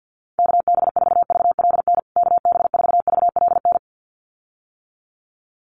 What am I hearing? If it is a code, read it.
Morse code, heard as a keyed tone, about 50 wpm, 700 Hz. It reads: KB4VLN KB4VLN
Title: USA Call Signs Repeated 2x 50wpm